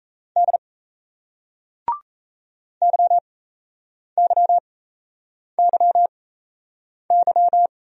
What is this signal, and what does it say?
Morse code, heard as a keyed tone, about 38 wpm, spaced as for 8 wpm, 700 Hz. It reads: D YYYY